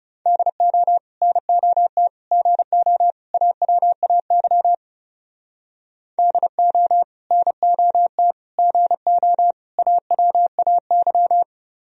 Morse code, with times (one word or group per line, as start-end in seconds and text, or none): 0.3-1.0 DO
1.2-2.1 NOT
2.3-3.1 GO
3.3-4.7 AWAY
6.2-7.0 DO
7.3-8.3 NOT
8.6-9.5 GO
9.8-11.4 AWAY